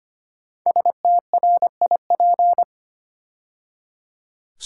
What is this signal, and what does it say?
Morse code, heard as a keyed tone, about 25 wpm, 700 Hz. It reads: STRIP
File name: 5 Letter Words - First 3 Letters Given - 25wpm